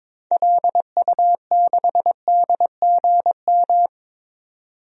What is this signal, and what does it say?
Morse code, heard as a keyed tone, about 22 wpm, 700 Hz. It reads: LU6DGM
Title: DXCC-100 Call Signs - 6 Characters - 22wpm